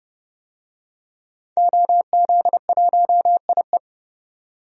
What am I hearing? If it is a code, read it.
Morse code, heard as a keyed tone, about 30 wpm, 700 Hz. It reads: OZ1IE
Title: DXCC-100 Call Signs - 5 Characters - 30wpm